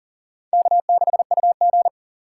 Morse code, heard as a keyed tone, about 50 wpm, 700 Hz. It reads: K6UG
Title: Speed Racing - USA Call Signs 40wpm